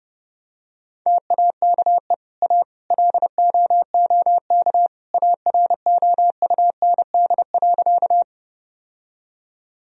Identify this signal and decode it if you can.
Morse code, heard as a keyed tone, about 30 wpm, 700 Hz. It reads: TAKE A LOOK AROUND.
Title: Sentences from Top 1000 Words - Review Part 3 - 30wpm